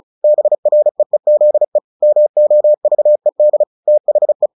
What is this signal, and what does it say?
Morse code, heard as a keyed tone, about 34 wpm, 600 Hz. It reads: BREEZE MOVED THE
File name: the old man and the sea 2 35wpm